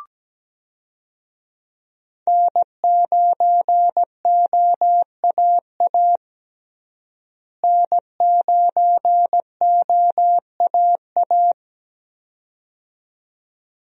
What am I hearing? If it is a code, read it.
Morse code, heard as a keyed tone, about 17 wpm, 700 Hz. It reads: N9OAA N9OAA
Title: USA Call Signs - 5 Characters - Repeated 2x - 17wpm